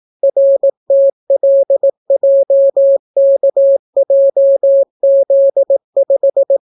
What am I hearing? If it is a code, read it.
Morse code, heard as a keyed tone, about 18 wpm, 550 Hz. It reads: RTLJKJZ5